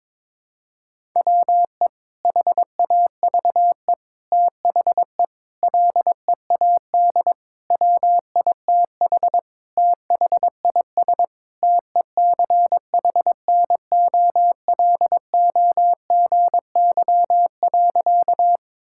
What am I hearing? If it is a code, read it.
Morse code, heard as a keyed tone, about 22 wpm, 700 Hz. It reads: WE HAVE THE LEAD WITH THIS TECHNOLOGY.
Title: Sentences from Top 1000 Words - Review Part 5 - 22wpm